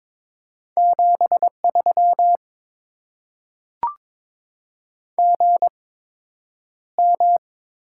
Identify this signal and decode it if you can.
Morse code, heard as a keyed tone, about 22 wpm, spaced as for 9 wpm, 700 Hz. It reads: 73 GM
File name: Top 20 QSO Elements - 8x Character Spacing - 22wpm